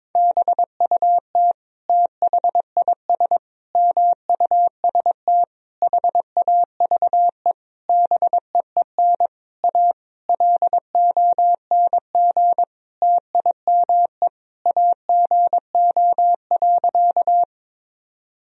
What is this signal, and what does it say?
Morse code, heard as a keyed tone, about 22 wpm, 700 Hz. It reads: BUT THIS MUST HAVE BEEN A LONG TIME AGO.
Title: Sentences from Top 1000 Words - Review Part 3 - 22wpm